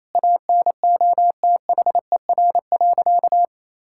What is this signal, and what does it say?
Morse code, heard as a keyed tone, about 27 wpm, 700 Hz. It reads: ANOTHER.
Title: Sentences from Top 1600 Words 28wpm